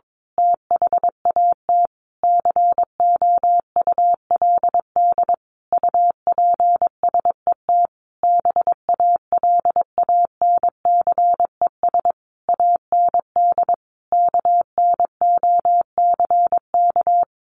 Morse code, heard as a keyed tone, about 22 wpm, 700 Hz. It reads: THAT COULD UPSET BALANCES AND KNOCK
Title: A Sound Of Thunder 22wpm